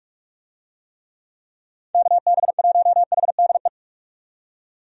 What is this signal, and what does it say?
Morse code, heard as a keyed tone, about 46 wpm, 700 Hz. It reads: KB1HDE
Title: USA Call Signs - 6 Characters - 45wpm